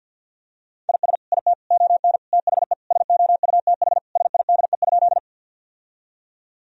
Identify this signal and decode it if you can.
Morse code, heard as a keyed tone, about 52 wpm, 700 Hz. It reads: IS IT ON THE SOUTH SIDE?
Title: Sentences from Top 1000 Words - Review Part 2 - 50wpm